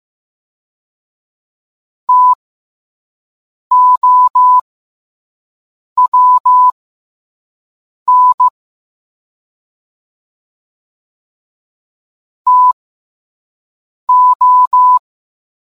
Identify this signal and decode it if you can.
Morse code, heard as a keyed tone, about 15 wpm, spaced as for 5 wpm, 1000 Hz. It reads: TOWN TO